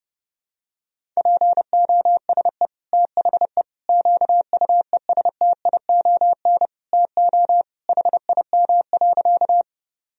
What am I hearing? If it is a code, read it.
Morse code, heard as a keyed tone, about 31 wpm, 700 Hz. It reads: POSE THE QUESTION TO HIM.
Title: Sentences from Top 1000 Words - Review Part 2 - 30wpm